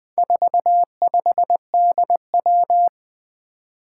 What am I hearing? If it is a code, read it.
Morse code, heard as a keyed tone, about 20 wpm, 700 Hz. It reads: 45DW
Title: Four Challenging Letter-Number Combinations 20wpm